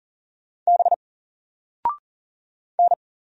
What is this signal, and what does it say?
Morse code, heard as a keyed tone, about 38 wpm, 700 Hz. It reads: B N
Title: Single Character (Letter:Number:Punctuation:Prosign) 40wpm